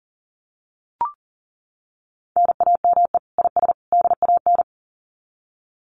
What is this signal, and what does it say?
Morse code, heard as a keyed tone, about 40 wpm, 700 Hz. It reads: NAME IS DAN
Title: Common QSO Phrases 40wpm